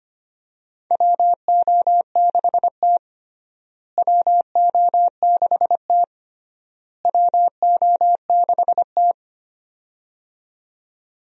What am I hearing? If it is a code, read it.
Morse code, heard as a keyed tone, about 25 wpm, 700 Hz. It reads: WO6T WO6T WO6T
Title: USA Call Signs - 4 Characters - Repeated 3x - 25wpm